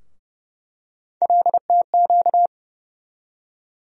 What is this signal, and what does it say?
Morse code, heard as a keyed tone, about 29 wpm, 700 Hz. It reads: LTQ
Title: USA Call Sign Postfixes - 3 Letters 30wpm